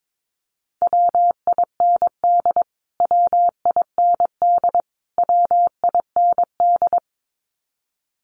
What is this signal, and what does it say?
Morse code, heard as a keyed tone, about 22 wpm, 700 Hz. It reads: WIND WIND WIND
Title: Top 500 Words Repeated 22wpm